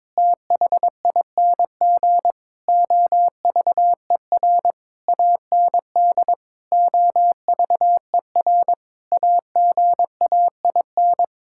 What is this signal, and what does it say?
Morse code, heard as a keyed tone, about 22 wpm, 700 Hz. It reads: THING OVER AND OVER AGAIN
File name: Sets of 9 Words - 22wpm